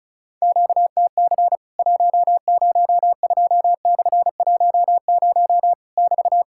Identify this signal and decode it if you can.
Morse code, heard as a keyed tone, about 34 wpm, 700 Hz. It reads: QTC 102/10 <BT>
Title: Worked All Europe DX Contest - QTC Exchange - 35wpm